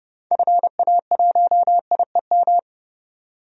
Morse code, heard as a keyed tone, about 30 wpm, 700 Hz. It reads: FA1IEM
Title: DXCC-100 Call Signs - 6 Characters - 30wpm